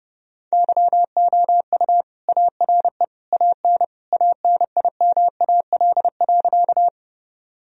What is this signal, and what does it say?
Morse code, heard as a keyed tone, about 30 wpm, 700 Hz. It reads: YOU ARE AN ANIMAL.
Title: Sentences from Top 1000 Words - Review Part 1 - 30wpm